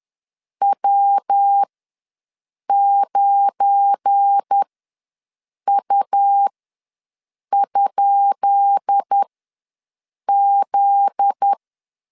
Morse code, heard as a keyed tone, about 10 wpm, spaced as for 6 wpm, 800 Hz. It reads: W9U?Z
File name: LEZIONI / 10 SIGNS / 10B INT ws 10wpm 4134